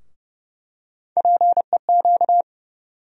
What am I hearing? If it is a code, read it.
Morse code, heard as a keyed tone, about 30 wpm, 700 Hz. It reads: PEQ
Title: USA Call Sign Postfixes - 3 Letters 30wpm